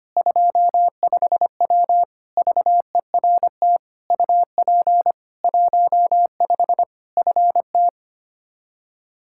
Morse code, heard as a keyed tone, about 25 wpm, 700 Hz. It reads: 25W VERT UP 15 FT
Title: Practice QSO - Exchange 2 - Info - 25wpm